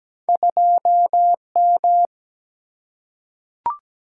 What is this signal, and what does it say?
Morse code, heard as a keyed tone, about 17 wpm, 700 Hz. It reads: 2M
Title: Single-Letter Single-Number 17wpm